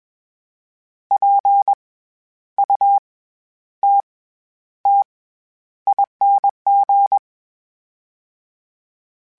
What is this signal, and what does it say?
Morse code, heard as a keyed tone, about 21 wpm, 800 Hz. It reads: P U T T ING
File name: Special INGs-mp3s / Special INGs-mp3s / SlowToFaster ing @20